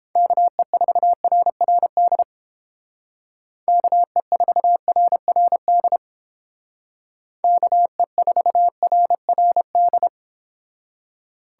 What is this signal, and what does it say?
Morse code, heard as a keyed tone, about 30 wpm, 700 Hz. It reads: KE4RRD KE4RRD KE4RRD